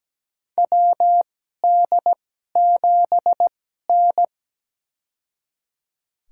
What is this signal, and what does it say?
Morse code, heard as a keyed tone, about 17 wpm, spaced as for 12 wpm, 700 Hz. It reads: WD7N
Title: USA Call Signs - 4 Characters - 2x Character Spacing - 17wpm